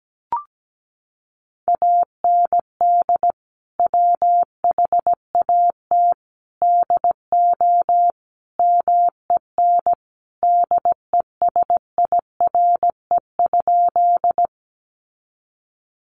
Morse code, heard as a keyed tone, about 17 wpm, 700 Hz. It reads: AND WHAT DO MEN DESIRE?
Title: Sentences from Top 1100 Words 17wpm